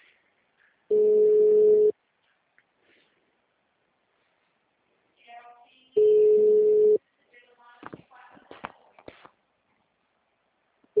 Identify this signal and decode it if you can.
Ringback tone (Europe, 425 Hz).